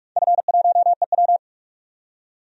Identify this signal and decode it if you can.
Morse code, heard as a keyed tone, about 42 wpm, 700 Hz. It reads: F1EW
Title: DXCC-100 Call Signs - 4 Characters - 45wpm